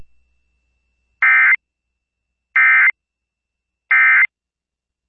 Emergency Alert System (SAME) tones, heard as harsh buzzing bursts. It is heard from 1.2 to 4.2 s.